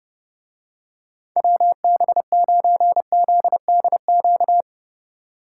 Morse code, heard as a keyed tone, about 29 wpm, 700 Hz. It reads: WB9ZDQ